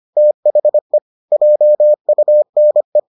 Morse code, heard as a keyed tone, about 25 wpm, 600 Hz. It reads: THE JUNE